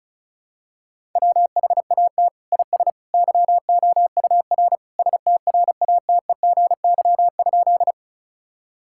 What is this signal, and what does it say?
Morse code, heard as a keyed tone, about 34 wpm, 690 Hz. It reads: WHAT IS YOUR STRATEGY?
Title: Sentences from Top 1000 Words - Review Part 5 - 35wpm